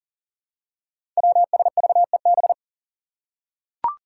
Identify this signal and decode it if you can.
Morse code, heard as a keyed tone, about 37 wpm, 700 Hz. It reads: WSVEB
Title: Five Challenging Letter-Number Combinations 40wpm